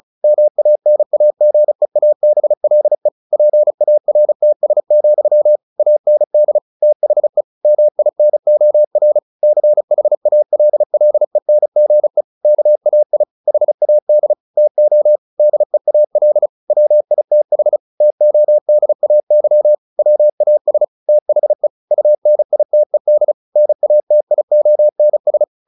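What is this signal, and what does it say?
Morse code, heard as a keyed tone, about 34 wpm, 600 Hz. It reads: MANAGEABLE PARTS, AND THE MINOR CHALLENGE KAI HAD TO DEAL WITH TODAY WAS THE UNITED NATIONS
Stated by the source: Morse practice at 35 wpm from CW med Gnister, sending Ken Follett's novel Never in English.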